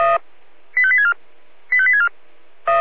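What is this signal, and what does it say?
Five-tone selective call (ZVEI), heard as quick runs of five tones, tones of about 70 ms. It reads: 85864 85864
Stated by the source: Received by SDR.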